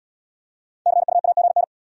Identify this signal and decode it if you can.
Morse code, heard as a keyed tone, about 64 wpm, 700 Hz. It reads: CUAGN